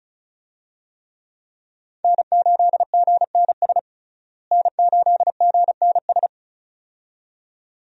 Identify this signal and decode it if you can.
Morse code, heard as a keyed tone, about 36 wpm, 700 Hz. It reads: N8GNS N8GNS
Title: USA Call Signs - 5 Characters - Repeated 2x - 35wpm